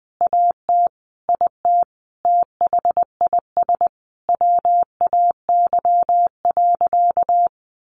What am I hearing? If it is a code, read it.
Morse code, heard as a keyed tone, about 20 wpm, 700 Hz. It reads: AT IT THIS WAY.
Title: Sentences from Top 100 Words 20wpm